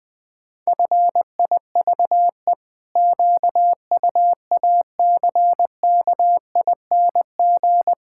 Morse code, heard as a keyed tone, about 20 wpm, 700 Hz. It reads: FIVE QUACKING
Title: Pangram Challenge v2 - 20wpm